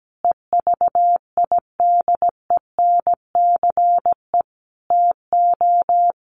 Morse code, heard as a keyed tone, about 17 wpm, 700 Hz. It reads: EVIDENCE TO
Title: Sentences from Top 1300 Words 17wpm